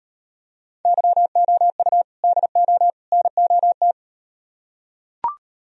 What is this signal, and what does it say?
Morse code, heard as a keyed tone, about 38 wpm, 700 Hz. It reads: YOU DO NOT